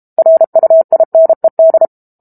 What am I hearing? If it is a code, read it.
Morse code, heard as a keyed tone, about 32 wpm, 650 Hz. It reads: RUINED
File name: Chapter 6 - 32 WPM